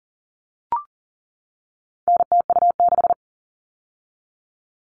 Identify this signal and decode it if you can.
Morse code, heard as a keyed tone, about 40 wpm, 700 Hz. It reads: NTU6
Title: All Cut Numbers - 4-Digit 40wpm